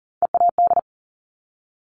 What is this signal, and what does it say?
Morse code, heard as a keyed tone, about 40 wpm, 690 Hz. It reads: EAD
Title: Top Three Letter Combinations in Words 40wpm